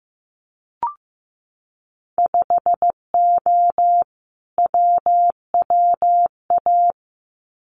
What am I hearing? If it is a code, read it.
Morse code, heard as a keyed tone, about 15 wpm, 700 Hz. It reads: 5O WWA
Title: Winter Field Day Exchanges Encore 15wpm